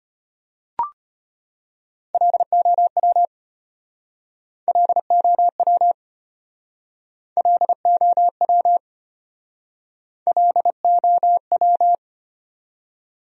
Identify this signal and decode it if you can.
Morse code, heard as a keyed tone, about 35 wpm, 700 Hz. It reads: LOW LOW LOW LOW